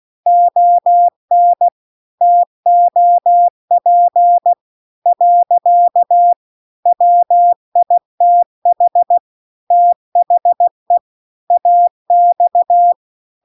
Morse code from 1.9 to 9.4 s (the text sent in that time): TOP . WITH